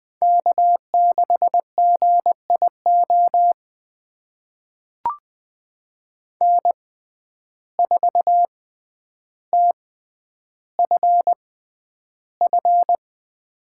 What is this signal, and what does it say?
Morse code, heard as a keyed tone, about 20 wpm, spaced as for 10 wpm, 700 Hz. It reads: K6GIO N4TFF
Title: USA Call Signs - 5 Characters - 6x Character Spacing - 20wpm